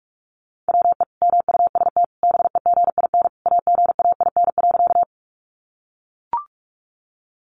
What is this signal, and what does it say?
Morse code, heard as a keyed tone, about 44 wpm, 700 Hz. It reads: WE MUST BEGIN AGAIN.
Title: Sentences from Top 300 Words 45wpm